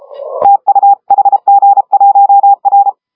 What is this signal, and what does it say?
Morse code, heard as a keyed tone, about 30 wpm, 810 Hz. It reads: TU HG1R
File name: HG1R CW SSB MIX